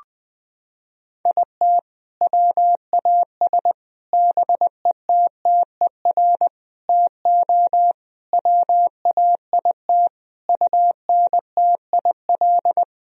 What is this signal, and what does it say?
Morse code, heard as a keyed tone, about 20 wpm, 700 Hz. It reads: IT WAS BETTER TO WAIT UNTIL
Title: Sentences from Top 1000 Words - Review Part 1 - 20wpm